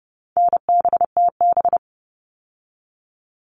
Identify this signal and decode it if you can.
Morse code, heard as a keyed tone, about 30 wpm, 700 Hz. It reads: NBTB
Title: All Cut Numbers - 4-Digit 30wpm